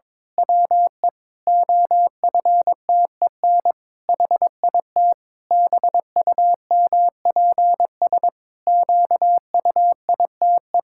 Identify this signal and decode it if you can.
Morse code, heard as a keyed tone, about 22 wpm, 700 Hz. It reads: WE OFTEN HIT BUMPS QUITE
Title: Pangram Challenge v2 - 22wpm